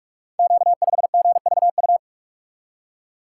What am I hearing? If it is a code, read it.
Morse code, heard as a keyed tone, about 44 wpm, 700 Hz. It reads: Q5GVU
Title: Code Groups - 5 Letter:Numbers - 45wpm